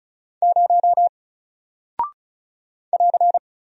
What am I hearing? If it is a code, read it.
Morse code, heard as a keyed tone, about 36 wpm, 700 Hz. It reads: 0 <AR>
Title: Single Character (Letter:Number:Punctuation:Prosign) Alt 35wpm